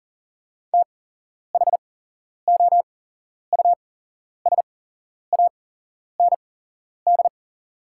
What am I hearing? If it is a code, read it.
Morse code, heard as a keyed tone, about 40 wpm, spaced as for 11 wpm, 700 Hz. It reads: THOUSAND
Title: 8x Character Spacing - Top 500 Words - 40wpm